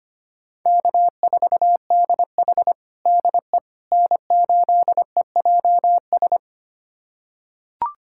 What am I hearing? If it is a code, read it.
Morse code, heard as a keyed tone, about 25 wpm, 700 Hz. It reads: K4DH DE N8EJS
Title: USA Callsign DE Callsign 25wpm